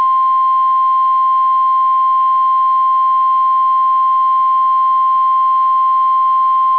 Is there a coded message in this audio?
Yes: Emergency Alert System (SAME) tones (the 1050 Hz attention tone) from 0.9 to 6.0 s.